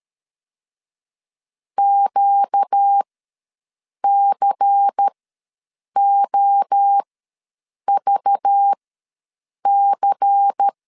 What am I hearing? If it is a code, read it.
Morse code, heard as a keyed tone, about 13 wpm, spaced as for 7 wpm, 800 Hz. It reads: QCOVC
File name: LEZIONI / 5 LBVC / 5 LBVC ws 12wpm 3159